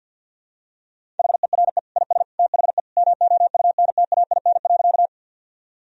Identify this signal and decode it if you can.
Morse code, heard as a keyed tone, about 52 wpm, 700 Hz. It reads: HERE IS THE MOUNTAIN.
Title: Sentences from Top 1000 Words - Review Part 1 - 50wpm